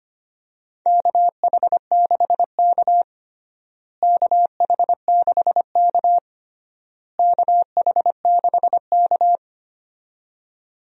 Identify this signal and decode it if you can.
Morse code, heard as a keyed tone, about 25 wpm, 700 Hz. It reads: KH6K KH6K KH6K
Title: DXCC-100 Call Signs - 4 Characters - Repeated 3x - 25wpm